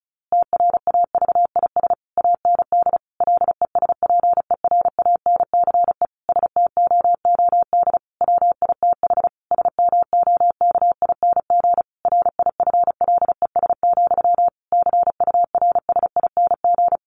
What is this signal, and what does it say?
Morse code, heard as a keyed tone, about 34 wpm, 700 Hz. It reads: TRAVIS AND LESPERANCE STOOD WITH SMOKING RIFLES, CURSING
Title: A Sound Of Thunder 35wpm